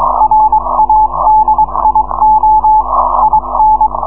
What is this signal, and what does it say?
Morse code, heard as a keyed tone, about 22 wpm, 860 Hz. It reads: RADIO IN